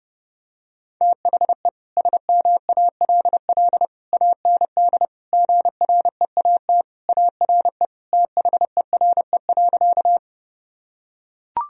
Morse code, heard as a keyed tone, about 30 wpm, 700 Hz. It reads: THE SMALL AND GREAT ARE THERE.
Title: Sentences from Top 1000 Words - Review Part 1 - 30wpm